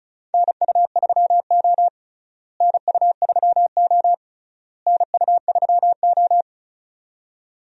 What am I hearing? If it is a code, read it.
Morse code, heard as a keyed tone, about 34 wpm, 700 Hz. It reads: NU3O NU3O NU3O
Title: USA Call Signs - 4 Characters - Repeated 3x - 35wpm